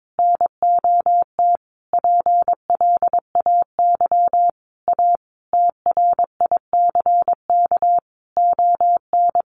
Morse code, heard as a keyed tone, about 22 wpm, 700 Hz. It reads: NOT PLAY A TRICK ON
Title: Sentences from Top 1500 Words 22wpm